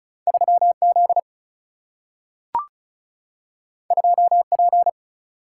Morse code, heard as a keyed tone, about 36 wpm, 700 Hz. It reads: 3Z 2P
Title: Single-Letter Single-Number 35wpm